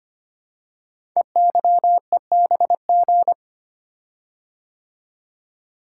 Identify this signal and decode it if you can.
Morse code, heard as a keyed tone, about 25 wpm, 700 Hz. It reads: EYEBG
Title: Code Groups - 5 Letters - 25wpm